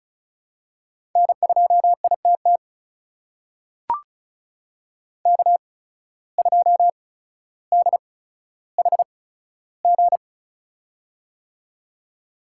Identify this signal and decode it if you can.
Morse code, heard as a keyed tone, about 34 wpm, spaced as for 15 wpm, 700 Hz. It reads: N2ITT K2DHG
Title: USA Call Signs - 8x Character Spacing - 35wpm